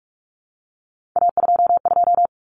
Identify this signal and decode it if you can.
Morse code, heard as a keyed tone, about 44 wpm, 700 Hz. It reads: A22
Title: Common Cut Numbers - 3-Digit 45wpm